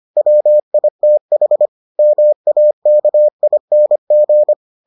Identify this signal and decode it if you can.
Morse code, heard as a keyed tone, about 25 wpm, 600 Hz. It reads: WITH MAKING